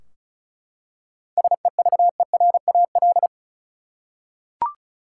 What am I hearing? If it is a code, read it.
Morse code, heard as a keyed tone, about 36 wpm, 700 Hz. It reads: SEVERAL
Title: Seven Letter Words - 35wpm